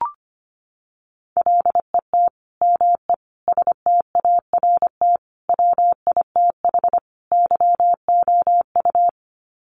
Morse code, heard as a keyed tone, about 25 wpm, 700 Hz. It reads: LET ME START WITH YOU
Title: Sets of 5 Words Encore 25wpm